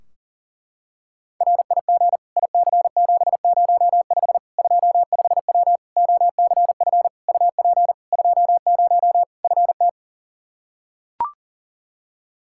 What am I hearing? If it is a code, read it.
Morse code, heard as a keyed tone, about 40 wpm, 700 Hz. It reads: RIG IC705 25W OCF UP 20 FT
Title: K1USN SST QSO Practice 40wpm